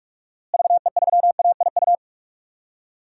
Morse code, heard as a keyed tone, about 46 wpm, 700 Hz. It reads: VE3AIU
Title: DXCC-100 Call Signs - 6 Characters - Repeated 2x - 45wpm